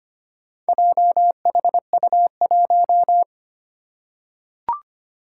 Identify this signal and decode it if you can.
Morse code, heard as a keyed tone, about 25 wpm, 700 Hz. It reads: JHU1
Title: Four Challenging Letter-Number Combinations 25wpm